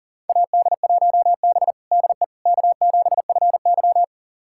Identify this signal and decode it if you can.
Morse code, heard as a keyed tone, about 40 wpm, 700 Hz. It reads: AD1B DE K7FY
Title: USA Callsign DE Callsign - 4 Characters - 40wpm